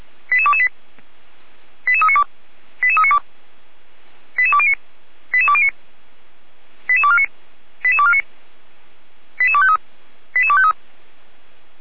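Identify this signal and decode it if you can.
Five-tone selective call (ZVEI), heard as quick runs of five tones, tones of about 70 ms. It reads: 88228 88392 88392 88209 88209 88249 88249 88253 88253
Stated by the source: ZVEI five-tone call 88193 received by SDR, recorded by Feuerwehr Orlamünde fire brigade.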